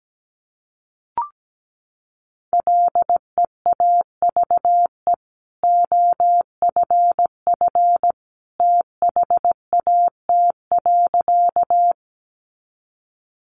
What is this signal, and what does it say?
Morse code, heard as a keyed tone, about 17 wpm, 700 Hz. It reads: LEAVE OFF THAT.
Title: Sentences from Top 400 Words 17wpm